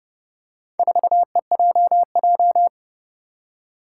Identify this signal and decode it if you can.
Morse code, heard as a keyed tone, about 30 wpm, 700 Hz. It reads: 4EJJ
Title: Four Challenging Letter-Number Combinations 30wpm